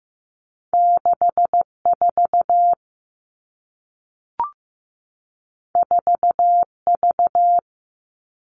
Morse code, heard as a keyed tone, about 15 wpm, 700 Hz. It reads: 6 4 4 V
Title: Two Challenging Letter-Number Combinations 15wpm